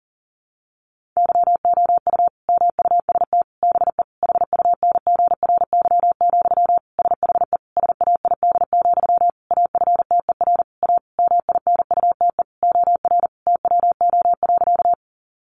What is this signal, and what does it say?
Morse code, heard as a keyed tone, about 40 wpm, 700 Hz. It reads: YOU MUST BE HUNGRY, SHE SAID, AFTER A MINUTE OR TWO.
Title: Sentences from Top 1400 Words 40wpm